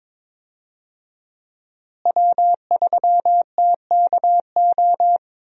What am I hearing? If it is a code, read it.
Morse code, heard as a keyed tone, about 22 wpm, 690 Hz. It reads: W3TKO
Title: USA Call Signs - 5 Characters - Repeated 3x - 22wpm